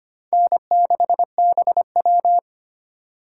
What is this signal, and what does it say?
Morse code, heard as a keyed tone, about 32 wpm, 700 Hz. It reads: N6BW